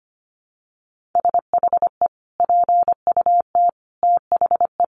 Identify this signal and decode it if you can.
Morse code, heard as a keyed tone, about 25 wpm, 700 Hz. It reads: SHE PUT THE
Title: Sentences from Top 1400 Words 25wpm